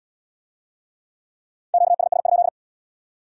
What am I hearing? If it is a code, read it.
Morse code, heard as a keyed tone, about 75 wpm, 700 Hz. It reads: QSS?